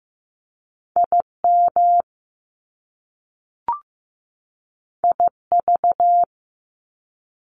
Morse code, heard as a keyed tone, about 15 wpm, 700 Hz. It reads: IM IV
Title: Top Two Letter Combinations in Words 15wpm